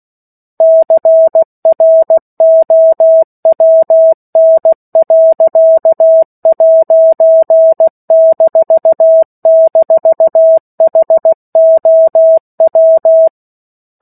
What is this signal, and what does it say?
Morse code, heard as a keyed tone, about 16 wpm, 650 Hz. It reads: CROWN.'--HOW
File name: 03-Chapter-03 16 WPM